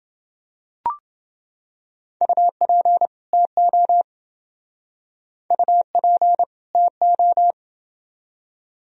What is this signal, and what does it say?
Morse code, heard as a keyed tone, about 27 wpm, 700 Hz. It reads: UP TO UP TO